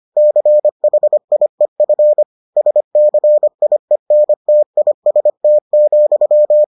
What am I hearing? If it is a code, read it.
Morse code, heard as a keyed tone, about 25 wpm, 600 Hz. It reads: CHIEF SCIENTIST,